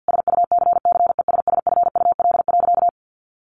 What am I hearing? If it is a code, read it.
Morse code, heard as a keyed tone, about 50 wpm, 700 Hz. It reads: SUCCESSFUL.
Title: Sentences from Top 1300 Words 50wpm